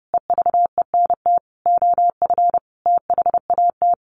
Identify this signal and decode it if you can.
Morse code, heard as a keyed tone, about 30 wpm, 700 Hz. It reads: EVENT OF THAT
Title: Sentences from Top 1400 Words 30wpm